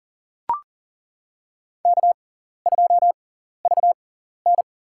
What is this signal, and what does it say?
Morse code, heard as a keyed tone, about 40 wpm, spaced as for 14 wpm, 700 Hz. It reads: K2VN
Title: USA Call Signs - 4 Characters - 6x Character Spacing - 40wpm